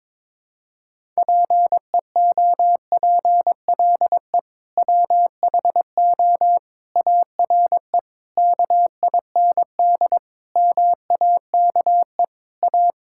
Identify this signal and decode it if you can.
Morse code, heard as a keyed tone, about 22 wpm, 700 Hz. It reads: PEOPLE WHO ARE KIND MAKE A
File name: practice / 22wpm / Sentences with 3-Word Phrases - 22wpm